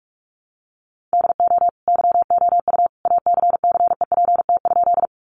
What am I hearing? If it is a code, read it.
Morse code, heard as a keyed tone, about 44 wpm, 700 Hz. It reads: DO YOU ACCEPT?
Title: Sentences from Top 1300 Words 45wpm